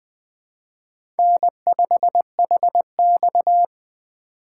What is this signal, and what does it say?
Morse code, heard as a keyed tone, about 21 wpm, 700 Hz. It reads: N5HX